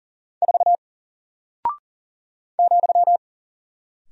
Morse code, heard as a keyed tone, about 40 wpm, 700 Hz. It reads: 4 ,
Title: Single Character (Letter:Number:Punctuation:Prosign) Alt 40wpm